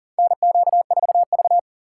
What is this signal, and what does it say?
Morse code, heard as a keyed tone, about 39 wpm, 700 Hz. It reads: NQ4V
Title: USA Call Signs - 4 Characters - 8x Character Spacing - 40wpm